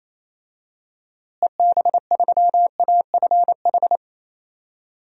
Morse code, heard as a keyed tone, about 28 wpm, 700 Hz. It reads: EB3AFH
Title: DXCC-100 Call Signs - 6 Characters - Repeated 2x - 28wpm